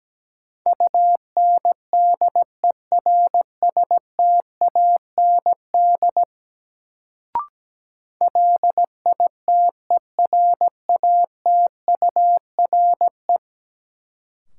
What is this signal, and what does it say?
Morse code, heard as a keyed tone, about 17 wpm, 700 Hz. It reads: UNDERSTAND LITERATURE
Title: Sentences from Top 600 Words - New Word Drill - 17wpm